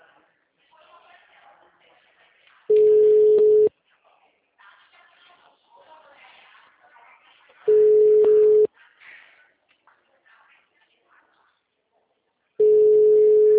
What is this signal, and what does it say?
Ringback tone (Europe, 425 Hz), with speech.
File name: out- -2025- - - .1667